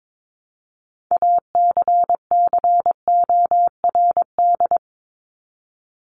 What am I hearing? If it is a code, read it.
Morse code, heard as a keyed tone, about 22 wpm, 700 Hz. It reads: ACCORD
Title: ING Suffix - Top 100 Words with Base Words 22wpm